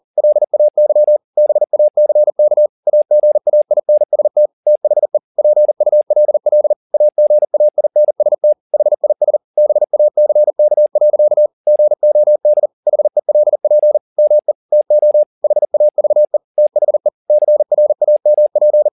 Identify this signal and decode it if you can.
Morse code, heard as a keyed tone, about 40 wpm, 600 Hz. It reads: LAY BACK AGAINST THE PULL AGAINST HIS BACK. GOD HELP ME TO HAVE THE CRAMP